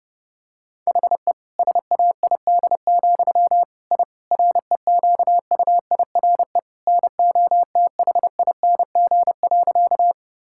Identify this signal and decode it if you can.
Morse code, heard as a keyed tone, about 28 wpm, 700 Hz. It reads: HE SAID, I REQUIRE NOTHING.
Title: Sentences from Top 1000 Words - Review Part 6 - 30wpm